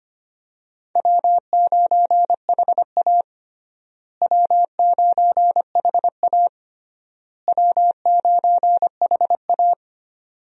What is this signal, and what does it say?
Morse code, heard as a keyed tone, about 25 wpm, 700 Hz. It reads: W9HA W9HA W9HA
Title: USA Call Signs - 4 Characters - Repeated 3x - 25wpm